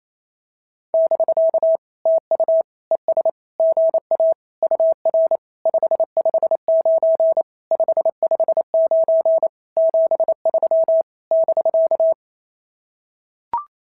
Morse code, heard as a keyed tone, about 28 wpm, 650 Hz. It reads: <BK> TU ES GA UR 559 559 73 <BK>
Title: SOTA QSO Practice 28wpm